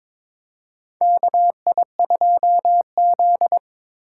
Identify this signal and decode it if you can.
Morse code, heard as a keyed tone, about 22 wpm, 700 Hz. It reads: KI2Z
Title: USA Call Signs - 4 Characters - 3x Character Spacing - 22wpm